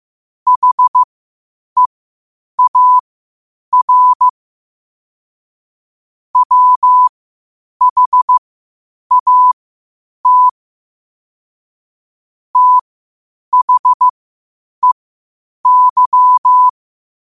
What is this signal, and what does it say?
Morse code, heard as a keyed tone, about 15 wpm, spaced as for 8 wpm, 1000 Hz. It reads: HEAR WHAT THEY
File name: qotdfiles / 8wpm